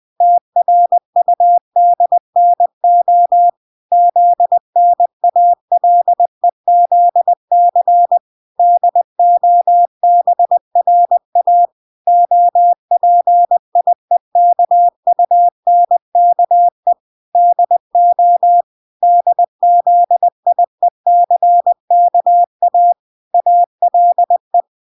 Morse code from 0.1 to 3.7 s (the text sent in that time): TRUDNO